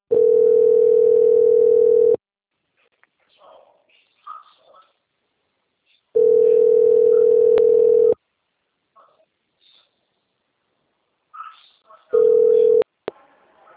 Ringback tone (North America), 440 + 480 Hz.
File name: out- - - - - .3758